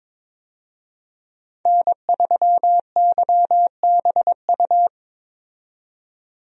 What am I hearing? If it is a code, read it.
Morse code, heard as a keyed tone, about 22 wpm, 700 Hz. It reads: N3YBU
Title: USA Call Signs - 5 Characters - 22wpm